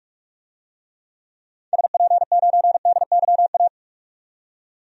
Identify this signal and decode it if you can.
Morse code, heard as a keyed tone, about 46 wpm, 700 Hz. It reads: SP9DYA
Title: DXCC-100 Call Signs - 6 Characters - 45wpm